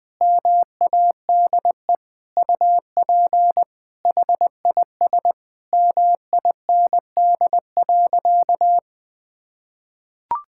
Morse code, heard as a keyed tone, about 20 wpm, 700 Hz. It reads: MADE UP HIS MIND.
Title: Sentences from Top 1000 Words - Review Part 5 - 20wpm